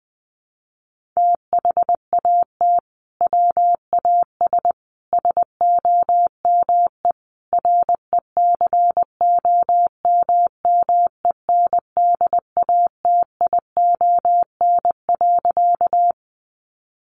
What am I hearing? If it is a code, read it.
Morse code, heard as a keyed tone, about 20 wpm, 700 Hz. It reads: THAT WAS SOME RECOMMENDATION.
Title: Sentences from Top 1400 Words 20wpm